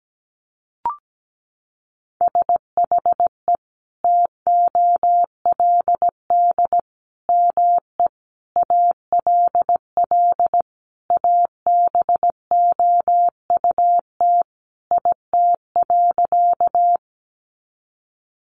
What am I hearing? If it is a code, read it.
Morse code, heard as a keyed tone, about 17 wpm, 700 Hz. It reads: SHE TOLD ME ALL ABOUT IT.
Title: Sentences from Top 400 Words 17wpm